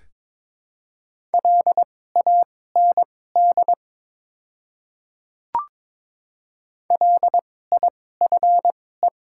Morse code, heard as a keyed tone, about 22 wpm, spaced as for 16 wpm, 700 Hz. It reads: LAND LIFE
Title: 2x Character Spacing - Top 300 Words - 22wpm